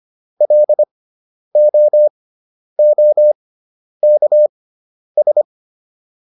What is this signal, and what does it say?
Morse code, heard as a keyed tone, about 25 wpm, spaced as for 10 wpm, 600 Hz. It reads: LOOKS